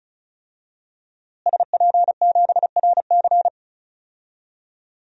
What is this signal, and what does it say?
Morse code, heard as a keyed tone, about 36 wpm, 700 Hz. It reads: SP7RC
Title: DXCC-100 Call Signs - 5 Characters - Repeated 2x - 35wpm